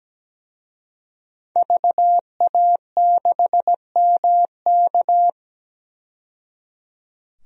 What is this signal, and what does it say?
Morse code, heard as a keyed tone, about 17 wpm, 700 Hz. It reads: VA6MK
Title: DXCC-100 Call Signs - 5 Characters - 17wpm